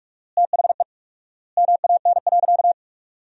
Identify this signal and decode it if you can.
Morse code, heard as a keyed tone, about 42 wpm, 700 Hz. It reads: THE MAN.
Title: 4x Word-Spacing - Sentences from Top 200 Words - 45wpm